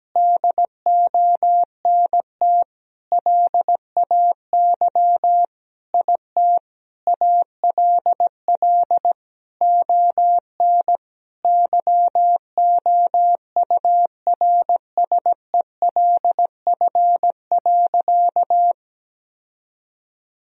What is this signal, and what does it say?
Morse code, heard as a keyed tone, about 17 wpm, 700 Hz. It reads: DONT LAY IT ALL ON YOURSELF.
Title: Sentences from Top 1000 Words - Review Part 2 - 17wpm